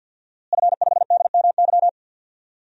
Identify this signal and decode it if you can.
Morse code, heard as a keyed tone, about 52 wpm, 700 Hz. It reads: F5DMY
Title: DXCC-100 Call Signs - 5 Characters - Repeated 2x - 50wpm